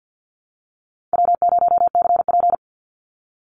Morse code, heard as a keyed tone, about 50 wpm, 700 Hz. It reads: W0CP
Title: NA SOTA Activator Callsigns 50wpm